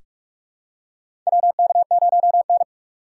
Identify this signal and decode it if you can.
Morse code, heard as a keyed tone, about 44 wpm, 700 Hz. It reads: WK0N
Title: USA Call Signs - 6x Character Spacing - 45wpm